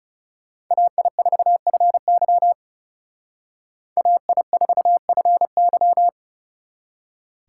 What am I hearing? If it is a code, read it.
Morse code, heard as a keyed tone, about 38 wpm, 700 Hz. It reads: AI4FY AI4FY